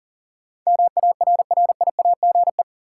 Morse code, heard as a keyed tone, about 40 wpm, 700 Hz. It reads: MARRIAGE
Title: Sentences from Top 800 Words - New Word Drill - 40wpm